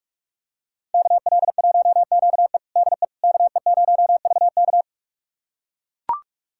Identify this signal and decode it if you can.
Morse code, heard as a keyed tone, about 43 wpm, 700 Hz. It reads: KL1QE DE KE0VK
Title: USA Callsign DE Callsign - 5 Characters - 45wpm